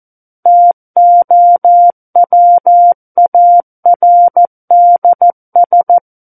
Morse code, heard as a keyed tone, about 14 wpm, 700 Hz. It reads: TOWARDS
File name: Book One / 05-Chapter 5 - 14 WPM